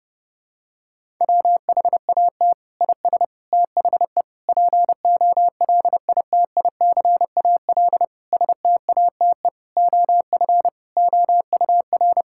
Morse code, heard as a keyed tone, about 32 wpm, 700 Hz. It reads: WHAT IS THE POLITICAL STATE OF OUR
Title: Sentences from Top 1000 Words - Review Part 5 - 30wpm